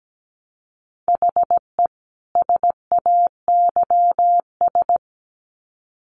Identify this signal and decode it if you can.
Morse code, heard as a keyed tone, about 17 wpm, 700 Hz. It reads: HE SAYS
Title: Sets of 2 Words 17wpm